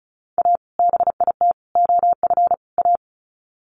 Morse code, heard as a keyed tone, about 34 wpm, 700 Hz. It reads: A BIT OF A
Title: Sets of 4 Words 35wpm